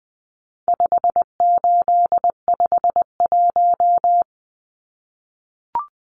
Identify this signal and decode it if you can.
Morse code, heard as a keyed tone, about 20 wpm, 700 Hz. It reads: 5851
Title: Numbers - 4-Digit 20wpm